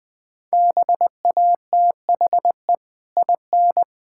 Morse code, heard as a keyed tone, about 20 wpm, 700 Hz. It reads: BATHE IN
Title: Easy Sentences with Hints - 20wpm